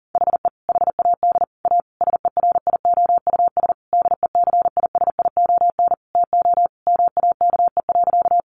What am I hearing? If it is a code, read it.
Morse code, heard as a keyed tone, about 40 wpm, 700 Hz. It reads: HE HAD A SERIOUS DECISION TO MAKE.
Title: Sentences from Top 1100 Words 40wpm